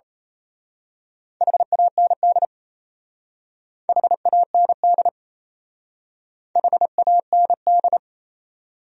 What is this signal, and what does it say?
Morse code, heard as a keyed tone, about 32 wpm, 700 Hz. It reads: HAND HAND HAND
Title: Speed-Racing Top 300 Words 28wpm